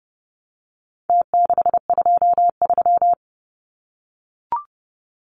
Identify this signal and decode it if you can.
Morse code, heard as a keyed tone, about 30 wpm, 700 Hz. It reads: T623
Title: Numbers - Contest Style - 4-Digit 30wpm